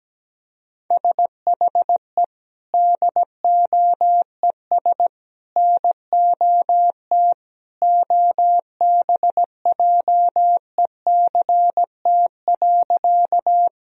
Morse code, heard as a keyed tone, about 17 wpm, 700 Hz. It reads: SHE DOES NOT OBJECT.
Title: Sentences from Top 1000 Words - Review Part 3 - 17wpm